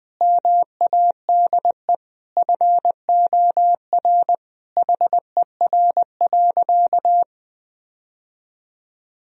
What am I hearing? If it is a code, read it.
Morse code, heard as a keyed tone, about 21 wpm, 700 Hz. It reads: MADE FOR HER.
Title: Sentences from Top 1000 Words - Review Part 3 - 20wpm